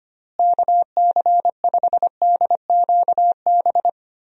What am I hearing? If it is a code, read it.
Morse code, heard as a keyed tone, about 25 wpm, 700 Hz. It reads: KC5DQB
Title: USA Call Signs - 6 Characters - Repeated 3x - 25wpm